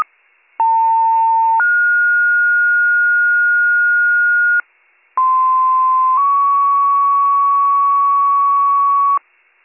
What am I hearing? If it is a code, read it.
Two-tone page, heard as a one-second tone, then a three-second tone, about 900 then 1430 Hz, about 1040 then 1120 Hz. It is heard from 0.6 to 9.2 s.